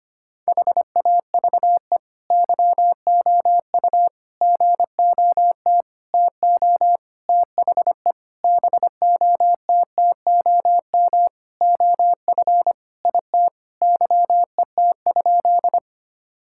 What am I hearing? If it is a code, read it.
Morse code, heard as a keyed tone, about 25 wpm, 700 Hz. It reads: HAVE YOU GOT TO THE BOTTOM OF IT YET?
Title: Sentences from Top 1000 Words - Review Part 6 - 25wpm